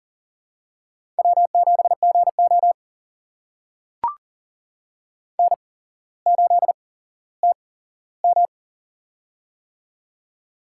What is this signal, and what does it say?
Morse code, heard as a keyed tone, about 40 wpm, spaced as for 17 wpm, 700 Hz. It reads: W7GO N8TM
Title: USA Call Signs - 4 Characters - 8x Character Spacing - 40wpm